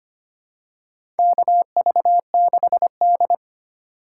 Morse code, heard as a keyed tone, about 25 wpm, 700 Hz. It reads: KV6D
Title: USA Call Signs - 4x Character Spacing - 25wpm